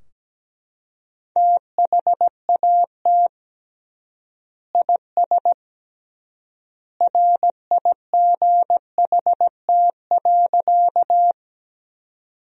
Morse code, heard as a keyed tone, about 17 wpm, 700 Hz. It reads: THAT IS RIGHT.
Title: 3x Word-Spacing - Sentences from Top 200 Words - 17wpm